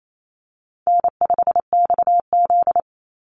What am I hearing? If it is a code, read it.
Morse code, heard as a keyed tone, about 28 wpm, 700 Hz. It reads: N5XZ
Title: Easy Call Signs Repeated 3x - 28wpm